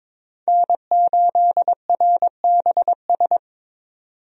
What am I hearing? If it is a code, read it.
Morse code, heard as a keyed tone, about 22 wpm, 700 Hz. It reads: N8RBS
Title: USA Call Signs - 5 Characters - 22wpm